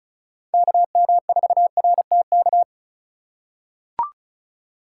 Morse code, heard as a keyed tone, about 36 wpm, 700 Hz. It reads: KM4RTK
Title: USA Call Signs - 6 Characters - 3x Character Spacing - 35wpm